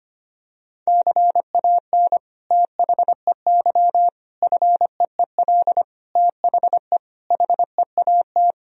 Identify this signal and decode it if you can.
Morse code, heard as a keyed tone, about 25 wpm, 700 Hz. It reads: CAN THEY FEEL THE HEAT
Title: Sentences from Top 1000 Words - Review Part 3 - 25wpm